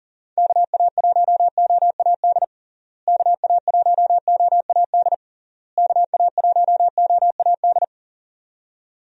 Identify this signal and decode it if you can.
Morse code, heard as a keyed tone, about 40 wpm, 700 Hz. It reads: KA1OAD KA1OAD KA1OAD
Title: USA Call Signs - 6 Characters - Repeated 3x - 40wpm